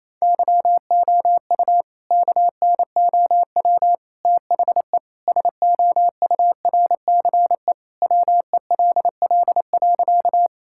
Morse code, heard as a keyed tone, about 28 wpm, 700 Hz. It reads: YOU KNOW THE SOURCE WELL.
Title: Sentences from Top 1000 Words - Review Part 5 - 28wpm